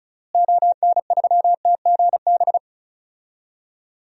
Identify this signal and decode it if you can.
Morse code, heard as a keyed tone, about 36 wpm, 700 Hz. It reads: ON3TGB
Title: DXCC-100 Call Signs - 6 Characters - Repeated 2x - 35wpm